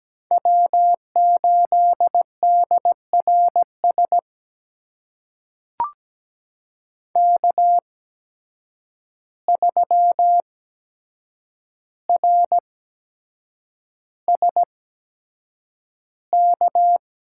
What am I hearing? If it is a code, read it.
Morse code, heard as a keyed tone, about 17 wpm, spaced as for 7 wpm, 700 Hz. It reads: W8DRS K3RSK
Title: USA Call Signs - 5 Characters - 8x Character Spacing - 17wpm